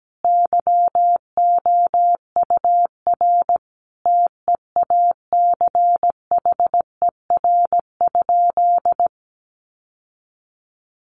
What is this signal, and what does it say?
Morse code, heard as a keyed tone, about 17 wpm, 700 Hz. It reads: YOUR TEACHER?
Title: Sentences from Top 1300 Words 17wpm